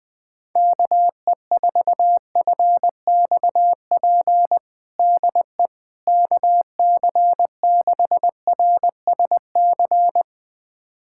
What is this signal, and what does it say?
Morse code, heard as a keyed tone, about 20 wpm, 700 Hz. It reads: KE4FXP DE KC6RSC
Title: USA Callsign DE Callsign - 6 Characters - 20wpm